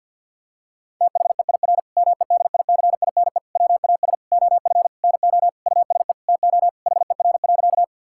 Morse code, heard as a keyed tone, about 50 wpm, 700 Hz. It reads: THEIR MEDICINE WAS OF NO USE TO HER.
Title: Sentences from Top 1000 Words - Review Part 6 - 50wpm